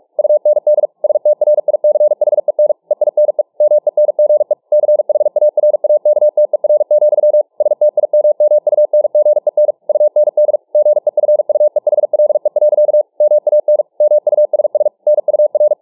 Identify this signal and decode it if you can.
Morse code, heard as a keyed tone, about 47 wpm, 600 Hz. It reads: UND STRICHEN EINE MENGE CHARAKTER, STIMMUNGEN UND GEFUEHLE. MAN MUSS NUR